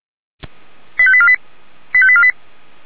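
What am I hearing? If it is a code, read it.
Five-tone selective call (ZVEI), heard as quick runs of five tones, tones of about 70 ms. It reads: 85848 85848
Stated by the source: ZVEI five-tone selective call received by SDR.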